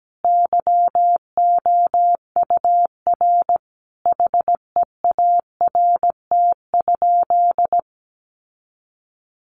Morse code, heard as a keyed tone, about 17 wpm, 700 Hz. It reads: YOUR HEART?
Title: Sentences from Top 1400 Words 17wpm